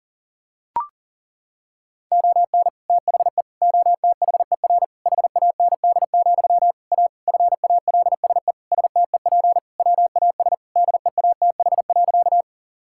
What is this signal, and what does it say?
Morse code, heard as a keyed tone, about 40 wpm, 700 Hz. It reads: ON THE OTHER HAND, A FALSE STEP WAS DEATH.
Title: Sentences from Top 1600 Words 40wpm